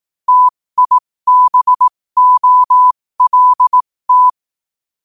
Morse code, heard as a keyed tone, about 18 wpm, spaced as for 15 wpm, 1000 Hz. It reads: TIBOLT
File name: qotdfiles / 15wpm